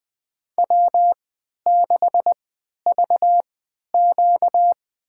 Morse code, heard as a keyed tone, about 21 wpm, spaced as for 11 wpm, 700 Hz. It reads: W6VQ
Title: USA Call Signs - 4 Characters - 3x Character Spacing - 20wpm